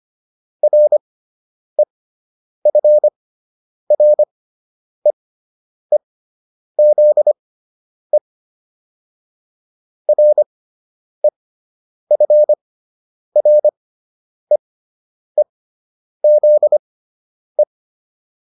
Morse code, heard as a keyed tone, about 25 wpm, spaced as for 9 wpm, 600 Hz. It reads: REFREEZE REFREEZE